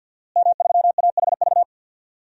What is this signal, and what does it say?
Morse code, heard as a keyed tone, about 48 wpm, 700 Hz. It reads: M3AHV
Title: DXCC-100 Call Signs - 5 Characters - Repeated 2x - 50wpm